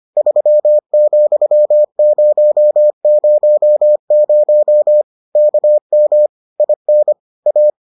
Morse code, heard as a keyed tone, about 25 wpm, 600 Hz. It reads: 3,000 KM IN A